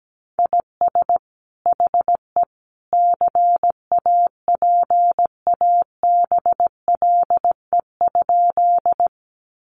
Morse code, heard as a keyed tone, about 17 wpm, 700 Hz. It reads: IS HE CAPABLE?
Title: Sentences from Top 1200 Words 17wpm